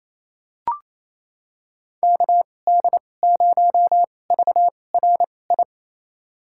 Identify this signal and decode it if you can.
Morse code, heard as a keyed tone, about 28 wpm, spaced as for 20 wpm, 700 Hz. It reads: KD0VRI
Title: USA Call Signs - 6 Characters - 2x Character Spacing - 28wpm